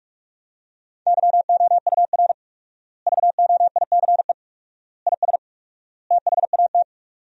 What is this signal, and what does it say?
Morse code, heard as a keyed tone, about 42 wpm, 700 Hz. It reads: YOUR VOICE IS THAT
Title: 4x Word-Spacing - Sentences from Top 500 Words - 45wpm